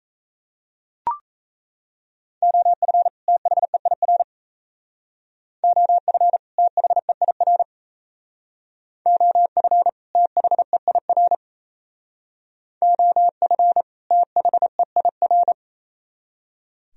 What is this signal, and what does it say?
Morse code, heard as a keyed tone, about 36 wpm, 700 Hz. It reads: OF THEIR OF THEIR OF THEIR OF THEIR